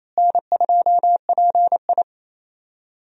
Morse code, heard as a keyed tone, about 27 wpm, 700 Hz. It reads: N2PI
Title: USA Call Signs - 4 Characters - Repeated 3x - 28wpm